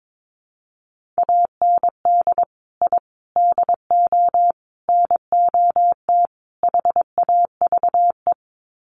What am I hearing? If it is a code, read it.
Morse code, heard as a keyed tone, about 22 wpm, 700 Hz. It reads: AND I DO NOT HAVE
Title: Sets of 5 Words Encore 22wpm